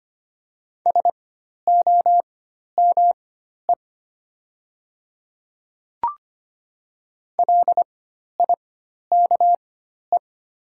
Morse code, heard as a keyed tone, about 25 wpm, spaced as for 12 wpm, 700 Hz. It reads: SOME LIKE
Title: 4x Character Spacing - Top 300 Words - 25wpm